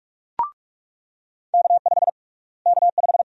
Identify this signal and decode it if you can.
Morse code, heard as a keyed tone, about 41 wpm, 700 Hz. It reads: K5 K5
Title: USA Call Sign Prefixes - Repeated 2x - 45wpm